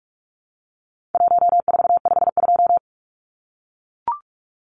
Morse code, heard as a keyed tone, about 44 wpm, 700 Hz. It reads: 1452
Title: Numbers - Contest Style - 4-Digit 45wpm